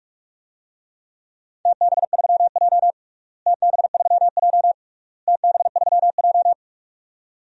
Morse code, heard as a keyed tone, about 41 wpm, 700 Hz. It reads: TB3J TB3J TB3J
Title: DXCC-100 Call Signs - 4 Characters - Repeated 3x - 45wpm